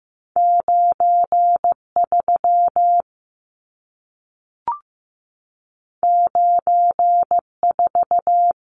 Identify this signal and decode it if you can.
Morse code, heard as a keyed tone, about 15 wpm, 700 Hz. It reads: 93 94
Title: Numbers - 2-Digit 15wpm